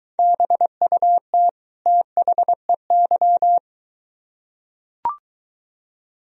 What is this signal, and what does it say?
Morse code, heard as a keyed tone, about 23 wpm, 700 Hz. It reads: BUT THEY